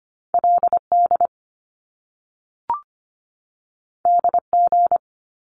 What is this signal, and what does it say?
Morse code, heard as a keyed tone, about 25 wpm, 700 Hz. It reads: LD DG
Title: Top Two Letter Combinations in Words 25wpm